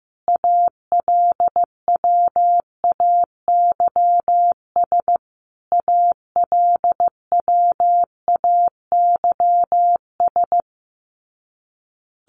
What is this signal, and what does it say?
Morse code, heard as a keyed tone, about 15 wpm, 700 Hz. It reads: ALWAYS ALWAYS
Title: Top 500 Words Repeated 15wpm